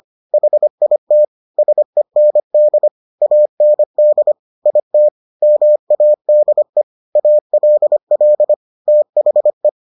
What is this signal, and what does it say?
Morse code, heard as a keyed tone, about 25 wpm, 600 Hz. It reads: HIT SEND AND IT MADE ALL THE